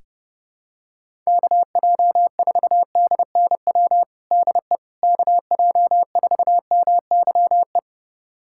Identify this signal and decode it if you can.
Morse code, heard as a keyed tone, about 30 wpm, 700 Hz. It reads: KJ4DNW DE KJ4MYE
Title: USA Callsign DE Callsign - 6 Characters - 30wpm